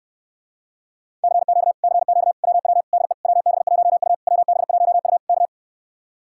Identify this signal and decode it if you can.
Morse code, heard as a keyed tone, about 69 wpm, 700 Hz. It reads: CQ CQ WW DE WB1U WB1U K